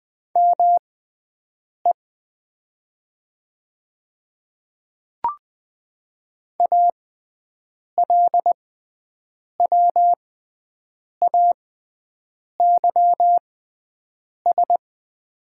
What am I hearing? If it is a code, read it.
Morse code, heard as a keyed tone, about 20 wpm, spaced as for 7 wpm, 700 Hz. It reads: ME ALWAYS
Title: 6x Character Spacing - Top 300 Words - 20wpm